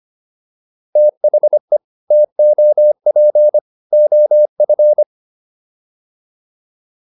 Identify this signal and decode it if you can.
Morse code, heard as a keyed tone, about 25 wpm, 600 Hz. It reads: THE TOP OF